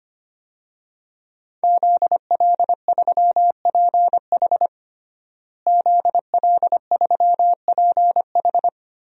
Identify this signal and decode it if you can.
Morse code, heard as a keyed tone, about 25 wpm, 700 Hz. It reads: ZL3PH ZL3PH
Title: DXCC-100 Call Signs - 5 Characters - Repeated 2x - 25wpm